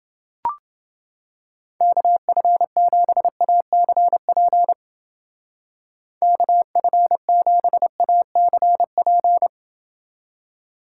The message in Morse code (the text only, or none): KF7ACP KF7ACP